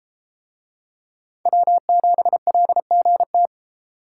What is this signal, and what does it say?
Morse code, heard as a keyed tone, about 32 wpm, 700 Hz. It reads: W7LGT